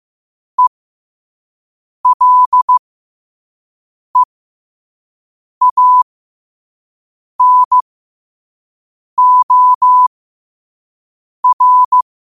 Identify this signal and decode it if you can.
Morse code, heard as a keyed tone, about 15 wpm, spaced as for 5 wpm, 1000 Hz. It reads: ELEANOR